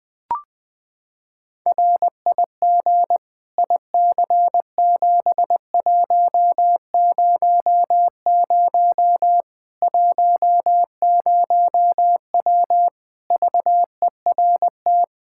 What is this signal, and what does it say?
Morse code, heard as a keyed tone, about 20 wpm, 700 Hz. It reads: RIG IC7100 10W VERT
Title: Practice QSO - Exchange 2 - Info - 20wpm